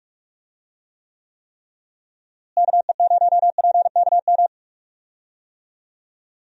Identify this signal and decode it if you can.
Morse code, heard as a keyed tone, about 46 wpm, 700 Hz. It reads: KE0PKM
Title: USA Call Signs - 6 Characters - Repeated 3x - 45wpm